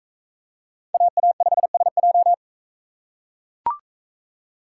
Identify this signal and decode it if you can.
Morse code, heard as a keyed tone, about 42 wpm, 700 Hz. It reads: AA5SJ